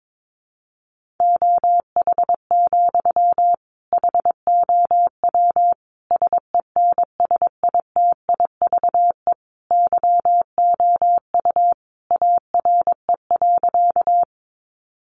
Morse code, heard as a keyed tone, about 22 wpm, 700 Hz. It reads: OH, HOW SENSITIVE YOU ARE.
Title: Sentences from Top 1400 Words 22wpm